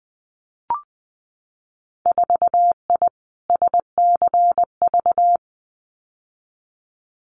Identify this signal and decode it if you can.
Morse code, heard as a keyed tone, about 20 wpm, 700 Hz. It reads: 4I SCV
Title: Winter Field Day Exchanges Encore 20wpm